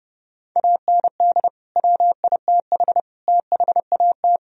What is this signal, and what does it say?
Morse code, heard as a keyed tone, about 32 wpm, 700 Hz. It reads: AND WITH THAT
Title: Sentences from Top 1000 Words - Review Part 1 - 30wpm